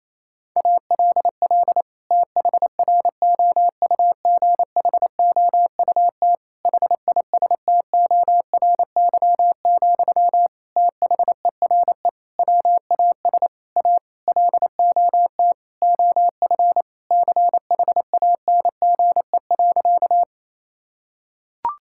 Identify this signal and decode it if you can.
Morse code, heard as a keyed tone, about 28 wpm, 700 Hz. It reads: ALL THROUGHOUT HISTORY, THERE WAS A LOT OF CHANGE.
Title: Sentences from Top 1000 Words - Review Part 5 - 28wpm